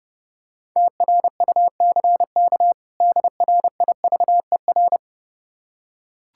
Morse code, heard as a keyed tone, about 30 wpm, 700 Hz. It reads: TRUCK DRIVER
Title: Practice QSO - Exchange 3 - More Info - 30wpm